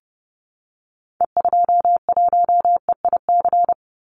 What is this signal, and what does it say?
Morse code, heard as a keyed tone, about 30 wpm, 700 Hz. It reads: E21EIC
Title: Call Signs - 30wpm